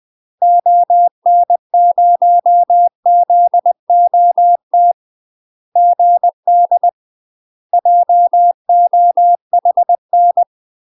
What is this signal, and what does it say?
Morse code, heard as a keyed tone, about 20 wpm, 700 Hz. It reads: ON0ZOT GD JOHN
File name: UK MASTER QSOs TRAINING / 20WPM / UK MASTER QSOs 0003